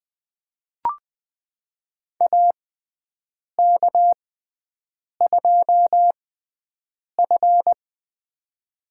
Morse code, heard as a keyed tone, about 20 wpm, spaced as for 7 wpm, 700 Hz. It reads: AK2F
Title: USA Call Signs - 4 Characters - 6x Character Spacing - 20wpm